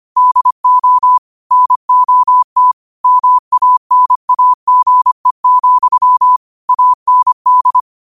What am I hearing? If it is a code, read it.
Morse code, heard as a keyed tone, about 25 wpm, 1000 Hz. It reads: DO NOT MANAGE, AND
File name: 25wpm